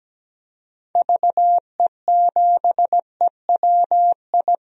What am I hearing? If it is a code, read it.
Morse code, heard as a keyed tone, about 17 wpm, 700 Hz. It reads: VE7EWI
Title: DXCC-100 Call Signs - 6 Characters - Repeated 2x - 17wpm